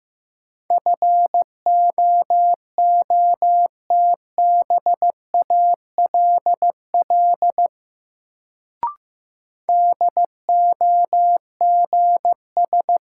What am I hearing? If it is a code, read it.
Morse code, heard as a keyed tone, about 15 wpm, 700 Hz. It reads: FOOTBALL DOGS
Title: Sentences from Top 1600 Words - New Word Drill - 15wpm